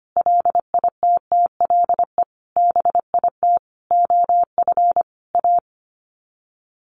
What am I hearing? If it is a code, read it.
Morse code, heard as a keyed tone, about 25 wpm, 700 Hz. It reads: LITTLE BIT OF A
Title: Sets of 5 Words 25wpm